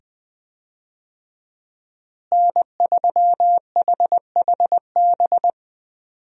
Morse code, heard as a keyed tone, about 20 wpm, 700 Hz. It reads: N3HHB
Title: USA Call Signs - 5 Characters - Repeated 2x - 20wpm